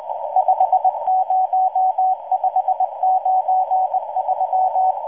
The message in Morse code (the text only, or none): 50593